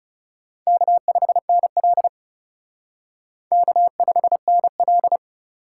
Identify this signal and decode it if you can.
Morse code, heard as a keyed tone, about 37 wpm, 700 Hz. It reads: K5NL K5NL
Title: Speed Racing - USA 1x2 Call Signs 30wpm